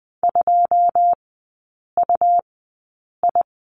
Morse code, heard as a keyed tone, about 20 wpm, spaced as for 8 wpm, 700 Hz. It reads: 2UI
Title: 100 Lesson 17e - Letter u - Three Characters 20wpm